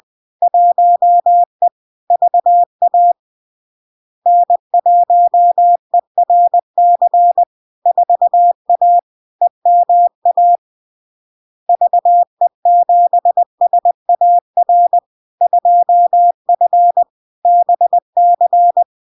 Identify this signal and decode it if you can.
Morse code, heard as a keyed tone, about 20 wpm, 700 Hz. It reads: 1E VA N1ERC 4A EMA VE7SAR 2F BC
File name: LARK-FD-2022-Log-in-CW-20wpm-700-Hz